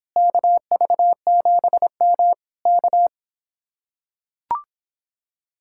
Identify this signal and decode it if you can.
Morse code, heard as a keyed tone, about 26 wpm, 700 Hz. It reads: KV7M K